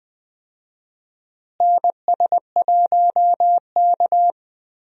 Morse code, heard as a keyed tone, about 20 wpm, 700 Hz. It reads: NS1K
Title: USA Call Signs - 4 Characters - Repeated 3x - 20wpm